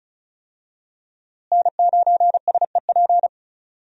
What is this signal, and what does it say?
Morse code, heard as a keyed tone, about 34 wpm, 700 Hz. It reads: N9SEP
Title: USA Call Signs - 5 Characters - Repeated 2x - 35wpm